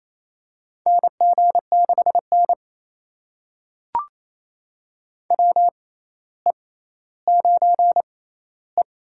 Morse code, heard as a keyed tone, about 28 wpm, spaced as for 14 wpm, 700 Hz. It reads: NG6N WE9E
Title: USA Call Signs - 4 Characters - 6x Character Spacing - 28wpm